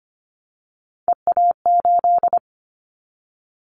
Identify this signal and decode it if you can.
Morse code, heard as a keyed tone, about 25 wpm, 700 Hz. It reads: EA8
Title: Callsign Prefixes for Top 50 DXCC - 25wpm